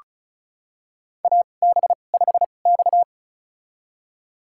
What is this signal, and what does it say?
Morse code, heard as a keyed tone, about 35 wpm, spaced as for 25 wpm, 700 Hz. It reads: AB5X
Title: USA Call Signs - 4 Characters - 2x Character Spacing - 35wpm